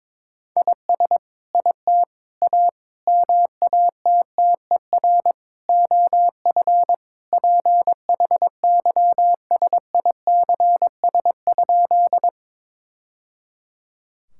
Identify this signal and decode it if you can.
Morse code, heard as a keyed tone, about 22 wpm, 700 Hz. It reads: IS IT A MATTER OF PHYSICS?
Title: Sentences from Top 1000 Words - Review Part 5 - 22wpm